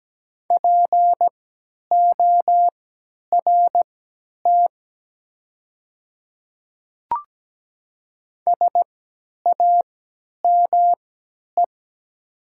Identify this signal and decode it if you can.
Morse code, heard as a keyed tone, about 17 wpm, spaced as for 10 wpm, 690 Hz. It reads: PORT SAME
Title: 3x Character Spacing - Top 200 Words - 17wpm